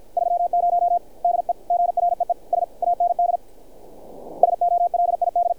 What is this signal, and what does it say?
Morse code, heard as a keyed tone, about 50 wpm, 700 Hz. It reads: 30 DEGREES AND SOLID
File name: archived / CRSnet